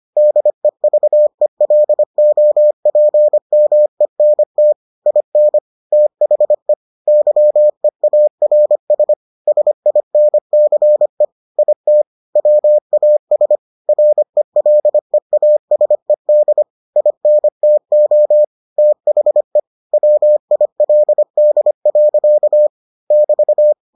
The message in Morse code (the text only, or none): DEVELOPMENT IN THE YEARS SINCE IT WAS RELEASED INTO THE WILD. <BT>